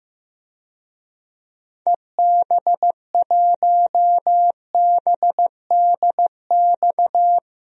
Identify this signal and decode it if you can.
Morse code, heard as a keyed tone, about 16 wpm, 700 Hz. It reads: EB1BDX
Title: DXCC-100 Call Signs - 6 Characters - Repeated 2x - 15wpm